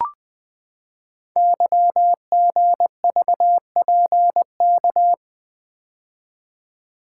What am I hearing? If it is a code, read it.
Morse code, heard as a keyed tone, about 20 wpm, 700 Hz. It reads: YGVPK
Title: Code Groups - 5 Letters - 20wpm